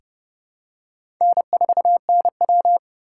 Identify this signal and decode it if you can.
Morse code, heard as a keyed tone, about 30 wpm, 700 Hz. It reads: N4NW
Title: K1USN SST QSO Practice - 30wpm